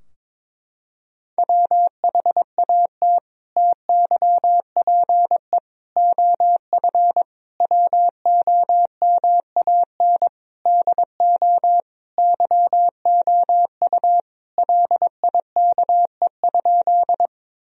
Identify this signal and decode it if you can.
Morse code, heard as a keyed tone, about 22 wpm, 700 Hz. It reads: WHAT TYPE OF WOMAN DO YOU LIKE?
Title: Sentences from Top 1000 Words - Review Part 3 - 22wpm